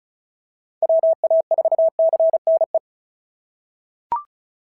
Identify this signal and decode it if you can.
Morse code, heard as a keyed tone, about 33 wpm, 660 Hz. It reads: WA4CNE
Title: SOTA QSO Practice 35wpm